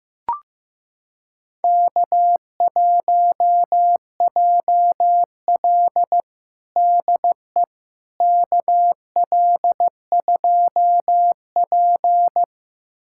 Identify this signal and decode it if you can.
Morse code, heard as a keyed tone, about 15 wpm, 700 Hz. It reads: K1JL DE KL2P
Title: USA Callsign DE Callsign - 4 Characters - 15wpm